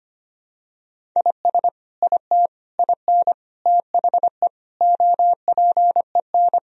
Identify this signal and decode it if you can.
Morse code, heard as a keyed tone, about 25 wpm, 700 Hz. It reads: IS IT IN THE OPEN
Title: Sentences from Top 1000 Words - Review Part 1 - 25wpm